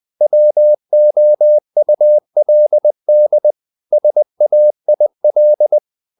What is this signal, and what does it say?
Morse code, heard as a keyed tone, about 20 wpm, 600 Hz. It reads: WOULD SAIL
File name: the old man and the sea part 2 20wpm